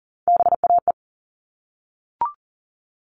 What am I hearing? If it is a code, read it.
Morse code, heard as a keyed tone, about 40 wpm, 700 Hz. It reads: BAE
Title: All Cut Numbers - 3-Digit 40wpm